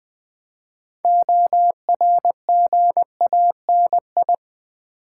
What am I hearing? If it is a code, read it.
Morse code, heard as a keyed tone, about 20 wpm, 700 Hz. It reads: ORGANI
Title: Build A Word - 6 Letters At A Time - 12-Letter Words - 20wpm